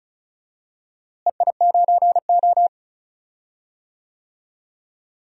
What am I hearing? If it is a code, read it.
Morse code, heard as a keyed tone, about 36 wpm, 700 Hz. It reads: EI9O
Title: DXCC-100 Call Signs - 4 Characters - Repeated 2x - 35wpm